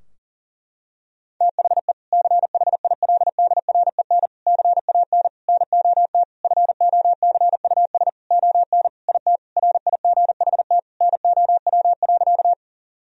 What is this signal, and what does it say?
Morse code, heard as a keyed tone, about 40 wpm, 700 Hz. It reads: THE CHILDREN CAN NOT FOCUS ON IT RIGHT NOW.
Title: Sentences from Top 1000 Words - Review Part 4 - 40wpm